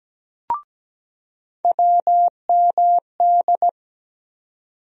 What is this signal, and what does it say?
Morse code, heard as a keyed tone, about 17 wpm, 700 Hz. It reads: WMD
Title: USA Call Sign Postfixes - 3 Letters 17wpm